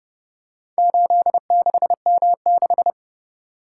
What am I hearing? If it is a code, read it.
Morse code, heard as a keyed tone, about 29 wpm, 700 Hz. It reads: 86M6